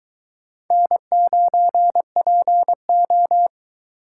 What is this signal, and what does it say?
Morse code, heard as a keyed tone, about 22 wpm, 700 Hz. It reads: N9PO